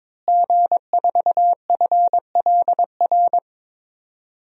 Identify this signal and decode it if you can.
Morse code, heard as a keyed tone, about 21 wpm, 700 Hz. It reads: G4FLR
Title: DXCC-100 Call Signs - 5 Characters - Repeated 3x - 22wpm